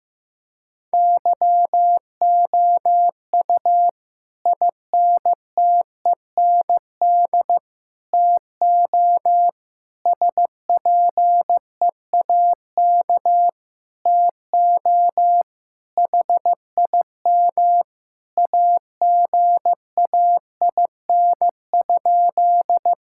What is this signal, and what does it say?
Morse code, heard as a keyed tone, about 15 wpm, 700 Hz. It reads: YOU INTEND TO SPEAK TO HIM AGAIN?
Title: Sentences from Top 1600 Words 15wpm